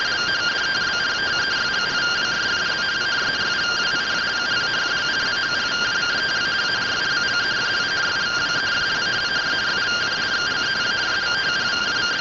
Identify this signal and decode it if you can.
Two-tone teleprinter signal, about 170 Hz shift.